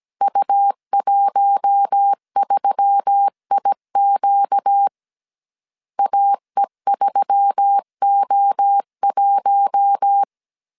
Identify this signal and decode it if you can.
Morse code, heard as a keyed tone, about 17 wpm, 800 Hz. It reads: U13IQ AE3O1
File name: 16wpm